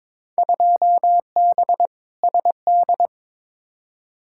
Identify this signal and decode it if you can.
Morse code, heard as a keyed tone, about 22 wpm, 700 Hz. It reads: 2B SD
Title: Field Day Exchanges Encore - 22wpm